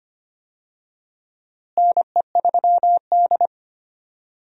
Morse code, heard as a keyed tone, about 25 wpm, 700 Hz. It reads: NE3D
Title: USA Call Signs - 4 Characters - Repeated 2x - 25wpm